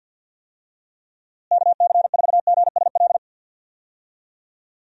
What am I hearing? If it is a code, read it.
Morse code, heard as a keyed tone, about 50 wpm, 700 Hz. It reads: KC4GSL
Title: USA Call Signs - 6 Characters - Repeated 2x - 50wpm